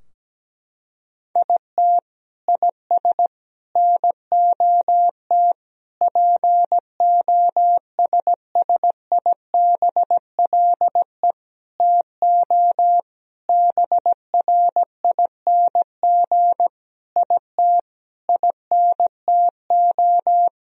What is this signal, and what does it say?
Morse code, heard as a keyed tone, about 17 wpm, 700 Hz. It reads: IT IS NOT POSSIBLE TO BRING IT INTO
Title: Sentences from Top 1000 Words - Review Part 4 - 17wpm